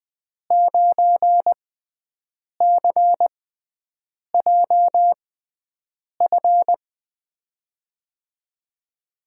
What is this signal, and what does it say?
Morse code, heard as a keyed tone, about 20 wpm, spaced as for 7 wpm, 700 Hz. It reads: 9CJF
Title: USA Call Signs - 5 Characters - 6x Character Spacing - 20wpm